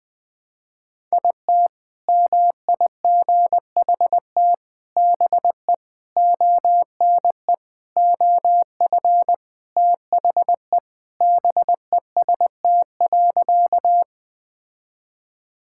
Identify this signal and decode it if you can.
Morse code, heard as a keyed tone, about 20 wpm, 700 Hz. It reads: IT MIGHT BE ONE OF THE BEST.
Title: Sentences with 4-Word Phrases - Spoken Beginning - 20wpm